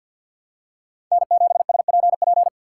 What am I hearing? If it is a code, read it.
Morse code, heard as a keyed tone, about 48 wpm, 700 Hz. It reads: N7SPP
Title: USA Call Signs - 5 Characters - 6x Character Spacing - 50wpm